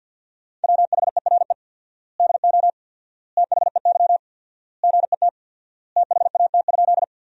Morse code, heard as a keyed tone, about 50 wpm, 700 Hz. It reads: WHERE DO THEY GET THAT?
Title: 4x Word-Spacing - Sentences from Top 200 Words - 50wpm